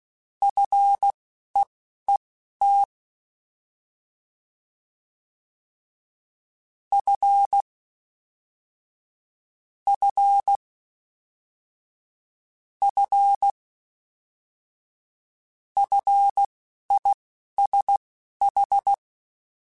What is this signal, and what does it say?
Morse code, heard as a keyed tone, about 16 wpm, spaced as for 11 wpm, 800 Hz. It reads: FEET F F F FISH
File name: Fs / Single F mp3s / Single F Prefixes 3x2 Twice @10